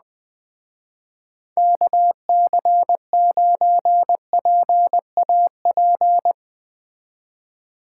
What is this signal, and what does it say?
Morse code, heard as a keyed tone, about 21 wpm, 700 Hz. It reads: KC9PAP
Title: Speed Racing - USA 2x3 Call Signs 17wpm